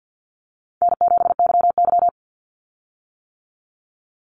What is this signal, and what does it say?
Morse code, heard as a keyed tone, about 50 wpm, 700 Hz. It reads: N7YY
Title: Easy Call Signs - 50wpm